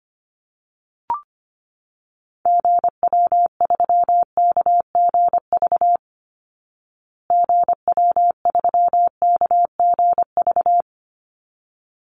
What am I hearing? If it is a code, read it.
Morse code, heard as a keyed tone, about 25 wpm, 700 Hz. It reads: GW3KGV GW3KGV
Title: Call Signs Repeated 3x - 25wpm